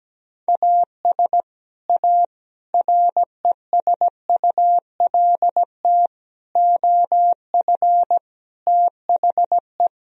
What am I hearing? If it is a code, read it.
Morse code, heard as a keyed tone, about 17 wpm, 700 Hz. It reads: AS A RESULT OF THE
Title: Speed-Racing - Sets of 5 Words 17wpm